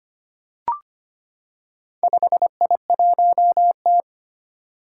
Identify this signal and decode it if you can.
Morse code, heard as a keyed tone, about 25 wpm, 700 Hz. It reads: 5I1T
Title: Four Challenging Letter-Number Combinations 25wpm